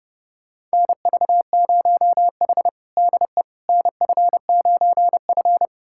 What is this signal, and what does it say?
Morse code, heard as a keyed tone, about 30 wpm, 700 Hz. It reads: NV0H DE NF9F
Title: USA Callsign DE Callsign - 4 Characters - 30wpm